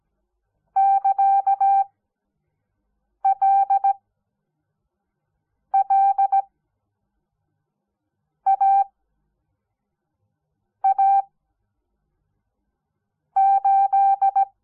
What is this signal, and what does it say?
Morse code, heard as a keyed tone, about 17 wpm, spaced as for 4 wpm, 780 Hz. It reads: <KA>LLAA8
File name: Onlinelehrgang / morsen / DJ4UF Morsen06